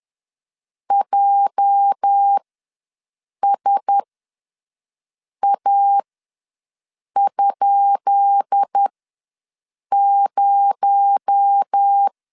Morse code, heard as a keyed tone, about 10 wpm, spaced as for 5 wpm, 800 Hz. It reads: JSA?0
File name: LEZIONI / 10 SIGNS / 10B INT ws 10wpm 4134